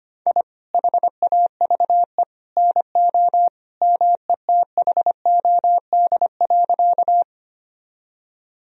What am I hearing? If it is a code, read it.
Morse code, heard as a keyed tone, about 25 wpm, 700 Hz. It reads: I HAVE NO METHOD.
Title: Sentences from Top 1000 Words - Review Part 3 - 25wpm